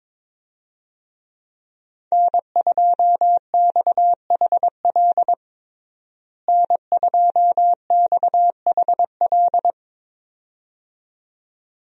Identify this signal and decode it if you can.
Morse code, heard as a keyed tone, about 22 wpm, 700 Hz. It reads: N2XHL N2XHL
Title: USA Call Signs - 5 Characters - Repeated 2x - 22wpm